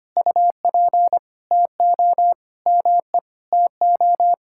Morse code, heard as a keyed tone, about 25 wpm, 700 Hz. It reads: UP TO ME TO
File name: practice / 25wpm / Sentences with 2-Word Phrases - 25wpm